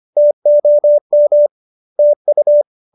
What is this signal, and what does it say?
Morse code, heard as a keyed tone, about 25 wpm, 570 Hz. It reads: TOM TU
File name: qso201b 25